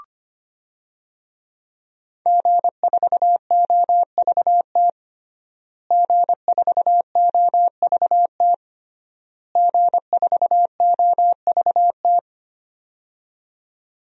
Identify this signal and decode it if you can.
Morse code, heard as a keyed tone, about 25 wpm, 700 Hz. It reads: G4OVT G4OVT G4OVT
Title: DXCC-100 Call Signs - 5 Characters - Repeated 3x - 25wpm